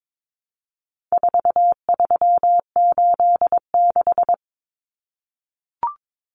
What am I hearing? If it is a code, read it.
Morse code, heard as a keyed tone, about 22 wpm, 700 Hz. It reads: 4386
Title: Common Cut Numbers - 4-Digit 22wpm